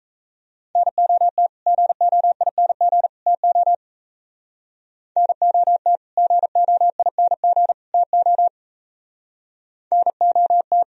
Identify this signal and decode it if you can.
Morse code, heard as a keyed tone, about 33 wpm, 700 Hz. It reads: NOT GOING TO NOT GOING TO NOT
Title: Speed-Racing - Sets of 3 Words 28wpm